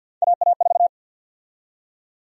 Morse code, heard as a keyed tone, about 53 wpm, 700 Hz. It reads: AA4
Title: USA Call Sign Prefixes - 50wpm